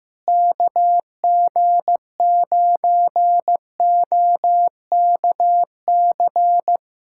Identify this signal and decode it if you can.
Morse code, heard as a keyed tone, about 15 wpm, 700 Hz. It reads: KG9OKC
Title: USA Callsign DE Callsign - 6 Characters - 15wpm